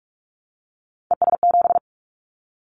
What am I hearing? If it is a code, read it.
Morse code, heard as a keyed tone, about 46 wpm, 700 Hz. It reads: ES7
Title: Callsign Prefixes for Top 50 DXCC - 45wpm